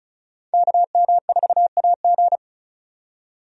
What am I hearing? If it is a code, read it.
Morse code, heard as a keyed tone, about 36 wpm, 700 Hz. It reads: KM4AG
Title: USA Call Signs - 5 Characters - 35wpm